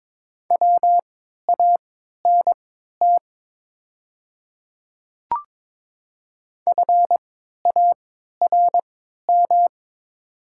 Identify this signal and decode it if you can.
Morse code, heard as a keyed tone, about 22 wpm, spaced as for 12 wpm, 690 Hz. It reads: WANT FARM
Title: 3x Character Spacing - Top 500 Words - 22wpm